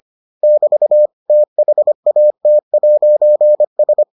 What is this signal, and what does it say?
Morse code, heard as a keyed tone, about 25 wpm, 600 Hz. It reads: <BT> THAT'S